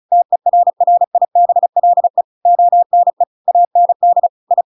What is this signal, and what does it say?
Morse code, heard as a keyed tone, about 34 wpm, 700 Hz. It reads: TERRIBLE ONE AND I
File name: Pinocchio / Pinocchio ch7 35 35 ik1hge